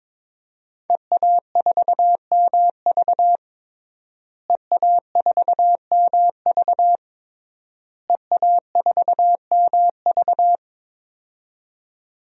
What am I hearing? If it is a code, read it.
Morse code, heard as a keyed tone, about 21 wpm, 700 Hz. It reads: EA4MV EA4MV EA4MV
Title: DXCC-100 Call Signs - 5 Characters - Repeated 3x - 22wpm